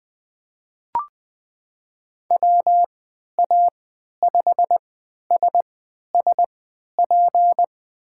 Morse code, heard as a keyed tone, about 21 wpm, spaced as for 11 wpm, 700 Hz. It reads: WA5SSP
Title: USA Call Signs - 6 Characters - 3x Character Spacing - 20wpm